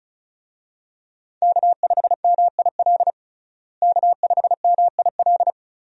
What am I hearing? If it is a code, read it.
Morse code, heard as a keyed tone, about 34 wpm, 700 Hz. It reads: K5MIL K5MIL
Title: USA Call Signs - 5 Characters - Repeated 2x - 35wpm